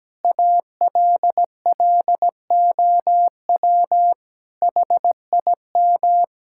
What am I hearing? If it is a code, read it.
Morse code, heard as a keyed tone, about 17 wpm, 700 Hz. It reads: ALLOW HIM
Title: Sentences from Top 1000 Words - Review Part 5 - 17wpm